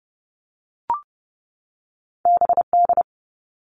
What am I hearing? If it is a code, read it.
Morse code, heard as a keyed tone, about 30 wpm, 700 Hz. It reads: BD
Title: All Cut Numbers - 2-Digit 30wpm